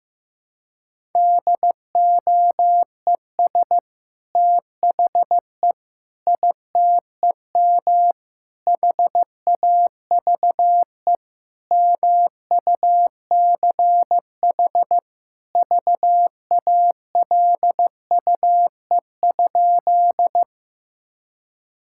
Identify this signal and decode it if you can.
Morse code, heard as a keyed tone, about 16 wpm, 700 Hz. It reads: DOES THE ITEM HAVE MUCH VALUE?
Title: Sentences from Top 1000 Words - Review Part 6 - 15wpm